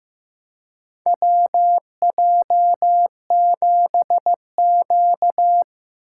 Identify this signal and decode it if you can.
Morse code, heard as a keyed tone, about 15 wpm, 700 Hz. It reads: WJ7Q
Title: USA Call Signs - 4 Characters - 3x Character Spacing - 15wpm